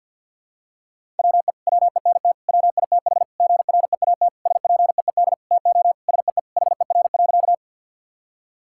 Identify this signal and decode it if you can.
Morse code, heard as a keyed tone, about 50 wpm, 700 Hz. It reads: WE WENT WITH GREAT SPEED TO SEE HER.
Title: Sentences from Top 1000 Words - Review Part 5 - 50wpm